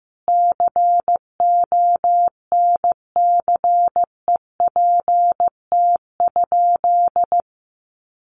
Morse code, heard as a keyed tone, about 15 wpm, 700 Hz. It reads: CONCEPT?
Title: Sentences from Top 1100 Words 15wpm